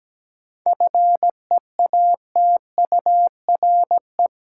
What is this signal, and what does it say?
Morse code, heard as a keyed tone, about 17 wpm, 700 Hz. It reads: FEATURE
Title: Sentences from Top 800 Words - New Word Drill - 17wpm